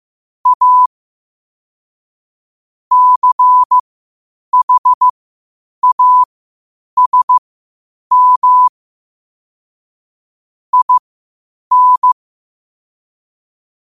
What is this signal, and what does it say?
Morse code, heard as a keyed tone, about 15 wpm, spaced as for 8 wpm, 1000 Hz. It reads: A CHASM IN
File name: qotdfiles / 8wpm